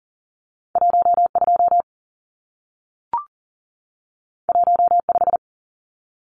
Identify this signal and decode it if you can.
Morse code, heard as a keyed tone, about 40 wpm, 700 Hz. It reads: 12 15
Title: Numbers - 2-Digit 40wpm